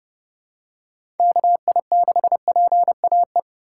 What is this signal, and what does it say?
Morse code, heard as a keyed tone, about 30 wpm, 700 Hz. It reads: KI6PAE
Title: USA Call Signs - 6 Characters - 3x Character Spacing - 30wpm